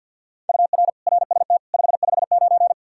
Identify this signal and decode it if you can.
Morse code, heard as a keyed tone, about 48 wpm, 700 Hz. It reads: UR RST 559
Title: Practice QSO - Exchange 1 - Beginning - 50wpm